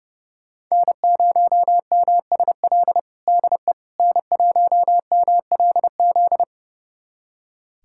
Morse code, heard as a keyed tone, about 30 wpm, 700 Hz. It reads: N0MSL DE N1MLZ
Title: USA Callsign DE Callsign - 5 Characters - 30wpm